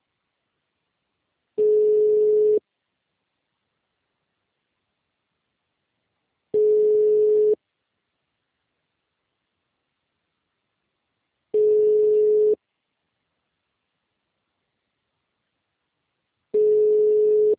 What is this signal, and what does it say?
Ringback tone (Europe, 425 Hz).